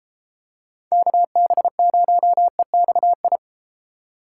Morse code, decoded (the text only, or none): KB0EXI